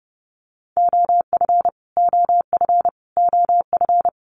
Morse code, heard as a keyed tone, about 30 wpm, 700 Hz. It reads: OF OF OF
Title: Top 500 Words Repeated 30wpm